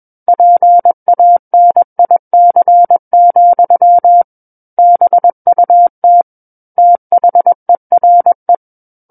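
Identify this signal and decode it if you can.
Morse code, heard as a keyed tone, about 21 wpm, 700 Hz. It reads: PANIC, BUT THERE
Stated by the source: Morse code practice reading of a book chapter at 21 words per minute.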